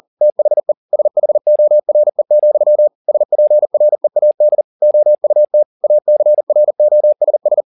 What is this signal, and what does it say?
Morse code, heard as a keyed tone, about 40 wpm, 600 Hz. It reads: THE SHORE, SPREAD OUT ACROSS